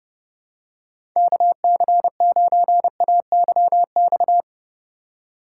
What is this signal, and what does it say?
Morse code, heard as a keyed tone, about 30 wpm, 700 Hz. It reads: KC9AYX
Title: USA Call Signs - 6 Characters - 3x Character Spacing - 30wpm